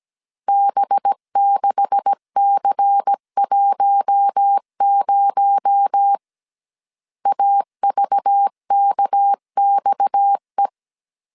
Morse code, heard as a keyed tone, about 17 wpm, 800 Hz. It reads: B6C10 AVKXE
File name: CW / DET / 1 9 1500C S A 16wpm 2828